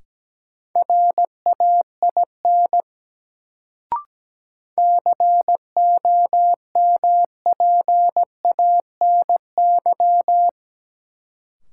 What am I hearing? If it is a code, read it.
Morse code, heard as a keyed tone, about 17 wpm, 700 Hz. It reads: RAIN COMPANY
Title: Sentences from Top 600 Words - New Word Drill - 17wpm